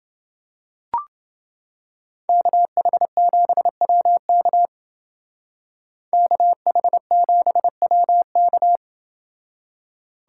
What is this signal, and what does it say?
Morse code, heard as a keyed tone, about 29 wpm, 700 Hz. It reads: KH7WK KH7WK